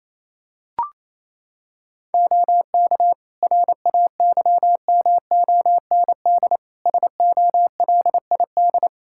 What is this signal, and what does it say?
Morse code, heard as a keyed tone, about 28 wpm, 700 Hz. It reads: OK RAYMOND SOLID
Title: Practice QSO - Exchange 3 - More Info - 28wpm